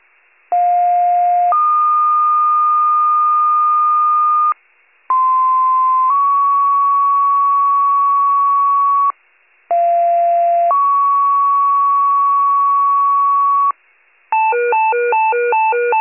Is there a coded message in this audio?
Yes: two-tone page from 0.5 to 13.7 s, heard as a one-second tone, then a three-second tone, about 710 then 1220 Hz, about 1040 then 1120 Hz, about 690 then 1120 Hz.